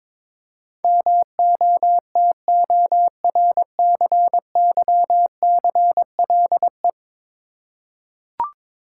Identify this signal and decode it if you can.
Morse code, heard as a keyed tone, about 22 wpm, 700 Hz. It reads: MOTORCYCLE
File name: practice / 22wpm / 10 Letter Words - First 3 Letters Given - 22wpm